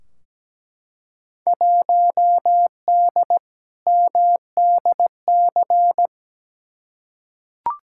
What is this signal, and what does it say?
Morse code, heard as a keyed tone, about 17 wpm, 700 Hz. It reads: 1D MDC
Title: Field Day Exchanges - 17wpm